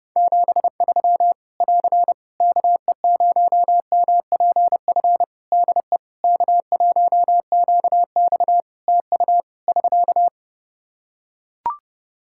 Morse code, heard as a keyed tone, about 30 wpm, 700 Hz. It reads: 73 <AR> KE0MPF DE K1QX TU <SK>
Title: Practice QSO - Exchange 4 - Ending - 30wpm